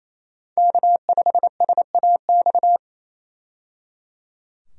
Morse code, heard as a keyed tone, about 28 wpm, 700 Hz. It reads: K5SAX
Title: USA Call Signs - 5 Characters - 28wpm